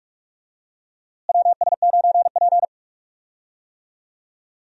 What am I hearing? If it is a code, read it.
Morse code, heard as a keyed tone, about 45 wpm, 700 Hz. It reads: WS9P
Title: USA Call Signs - 4 Characters - Repeated 2x - 45wpm